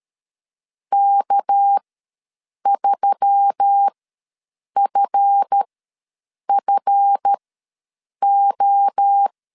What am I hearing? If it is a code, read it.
Morse code, heard as a keyed tone, about 13 wpm, spaced as for 7 wpm, 800 Hz. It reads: K3FFO
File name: LEZIONI / 7 GF9Z / 7 GF ws 12wpm 3140